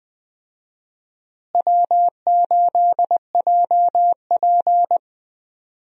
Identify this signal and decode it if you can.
Morse code, heard as a keyed tone, about 21 wpm, 700 Hz. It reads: W8JP
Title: USA Call Signs - 4 Characters - Repeated 3x - 20wpm